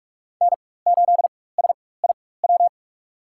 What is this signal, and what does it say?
Morse code, heard as a keyed tone, about 46 wpm, spaced as for 21 wpm, 700 Hz. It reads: N8SIW
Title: USA Call Signs - 5 Characters - 4x Character Spacing - 45wpm